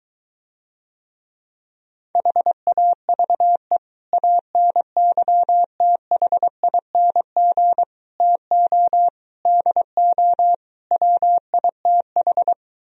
Morse code, heard as a keyed tone, about 24 wpm, 700 Hz. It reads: HAVE ANYTHING TO DO WITH